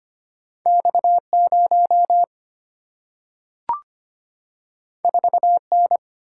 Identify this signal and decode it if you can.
Morse code, heard as a keyed tone, about 25 wpm, 700 Hz. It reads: X0 4N
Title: Single-Letter Single-Number 25wpm